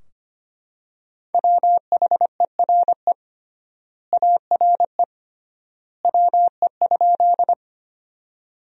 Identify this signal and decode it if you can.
Morse code, heard as a keyed tone, about 25 wpm, 700 Hz. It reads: WHERE ARE WE?
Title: 3x Word-Spacing - Sentences from Top 200 Words - 25wpm